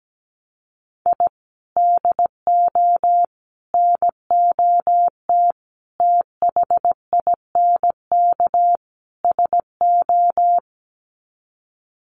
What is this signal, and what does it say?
Morse code, heard as a keyed tone, about 17 wpm, 700 Hz. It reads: I DO NOT THINK SO
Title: Sets of 5 Words 17wpm